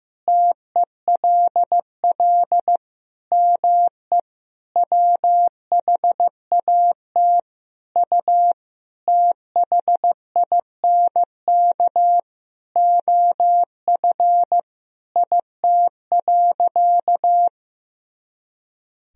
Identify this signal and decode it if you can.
Morse code, heard as a keyed tone, about 15 wpm, 700 Hz. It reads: TELL ME WHAT U THINK OF IT.
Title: Sentences from Top 200 Words and Abbreviations 15wpm